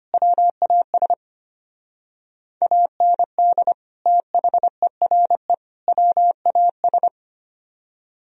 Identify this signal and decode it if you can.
Morse code, heard as a keyed tone, about 33 wpm, 700 Hz. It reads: WAS AND THERE WAS